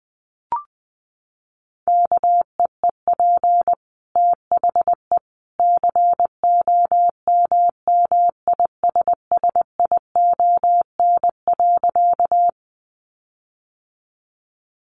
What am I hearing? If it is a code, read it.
Morse code, heard as a keyed tone, about 20 wpm, 700 Hz. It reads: KEEP THE COMMISSION.
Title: Sentences from Top 1400 Words 20wpm